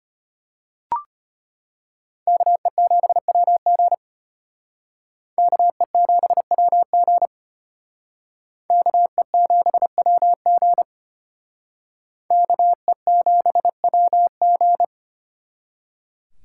Morse code, heard as a keyed tone, about 32 wpm, 700 Hz. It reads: KE7WG KE7WG KE7WG KE7WG